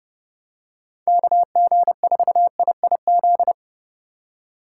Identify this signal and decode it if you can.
Morse code, heard as a keyed tone, about 30 wpm, 700 Hz. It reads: KG4IIZ
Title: USA Call Signs - 6 Characters - 3x Character Spacing - 30wpm